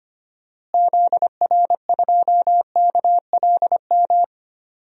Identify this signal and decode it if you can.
Morse code, heard as a keyed tone, about 25 wpm, 700 Hz. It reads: ZR2KLM
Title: DXCC-100 Call Signs - 6 Characters - Repeated 2x - 25wpm